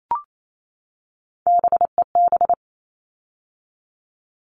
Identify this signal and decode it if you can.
Morse code, heard as a keyed tone, about 28 wpm, 700 Hz. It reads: BEB
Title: All Cut Numbers - 3-Digit 28wpm